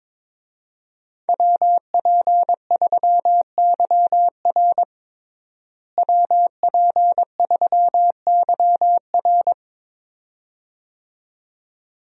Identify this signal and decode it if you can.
Morse code, heard as a keyed tone, about 22 wpm, 700 Hz. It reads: WP3YR WP3YR
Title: USA Call Signs - 5 Characters - Repeated 2x - 22wpm